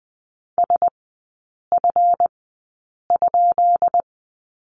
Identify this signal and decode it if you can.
Morse code, heard as a keyed tone, about 20 wpm, spaced as for 8 wpm, 700 Hz. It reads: SF?
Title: 138 Lesson 23e - Letter f - Three Characters 20wpm